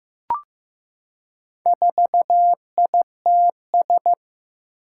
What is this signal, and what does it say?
Morse code, heard as a keyed tone, about 15 wpm, 700 Hz. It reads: 4ITS
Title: Four Challenging Letter-Number Combinations 15wpm